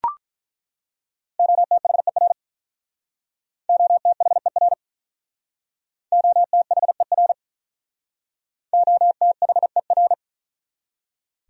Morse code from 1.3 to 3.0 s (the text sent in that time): OTHER